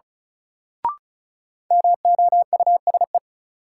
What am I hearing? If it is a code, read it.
Morse code, heard as a keyed tone, about 36 wpm, 700 Hz. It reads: MOUSE
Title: Sentences from Top 700 Words - New Word Drill - 35wpm